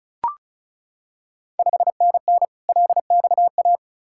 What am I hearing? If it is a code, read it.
Morse code, heard as a keyed tone, about 36 wpm, 700 Hz. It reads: 5NN LXA
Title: Ontario QSO Party Exchange 35wpm